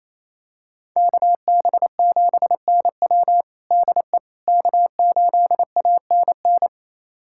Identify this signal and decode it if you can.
Morse code, heard as a keyed tone, about 28 wpm, 700 Hz. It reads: KB7NW DE K8ANN
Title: USA Callsign DE Callsign - 5 Characters - 28wpm